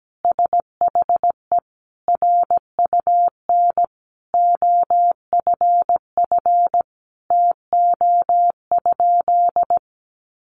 Morse code, heard as a keyed tone, about 17 wpm, 700 Hz. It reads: SHE RUN OFF TO?
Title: Sentences from Top 300 Words 17wpm